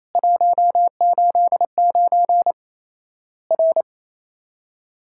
Morse code, heard as a keyed tone, about 27 wpm, 700 Hz. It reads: 189 R
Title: Worked All Europe DX Contest - QTC Exchange - 28wpm